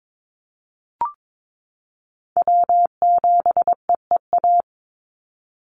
Morse code, heard as a keyed tone, about 22 wpm, 700 Hz. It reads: W7EEA
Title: USA Call Signs 22wpm